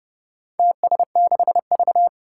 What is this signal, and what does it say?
Morse code, heard as a keyed tone, about 30 wpm, 700 Hz. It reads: TS6V
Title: Four Challenging Letter-Number Combinations 30wpm